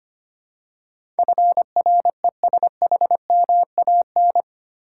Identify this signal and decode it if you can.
Morse code, heard as a keyed tone, about 25 wpm, 700 Hz. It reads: FRESHMAN
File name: 8 Letter Words - First 3 Letters Given - 25wpm